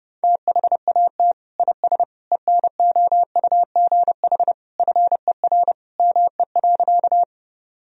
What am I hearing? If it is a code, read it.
Morse code, heard as a keyed tone, about 30 wpm, 700 Hz. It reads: THAT IS ENOUGH FER ME.
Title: Sentences from Top 400 Words and Abbreviations 30wpm